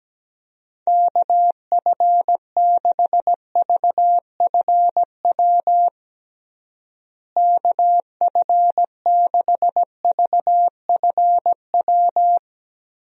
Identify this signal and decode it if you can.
Morse code, heard as a keyed tone, about 17 wpm, 700 Hz. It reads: KF6VFW KF6VFW
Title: USA Call Signs - 6 Characters - Repeated 3x - 17wpm